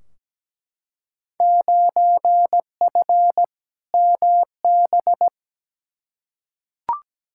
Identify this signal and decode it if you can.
Morse code, heard as a keyed tone, about 17 wpm, 700 Hz. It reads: 9F MB
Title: Field Day Exchanges Encore - 17wpm